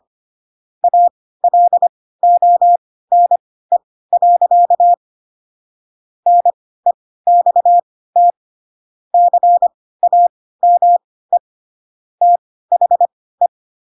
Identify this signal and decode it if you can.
Morse code, heard as a keyed tone, about 25 wpm, spaced as for 16 wpm, 700 Hz. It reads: ALONE. NEXT CAME THE
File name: Pinocchio / Pinocchio ch3 10 25 ik1hge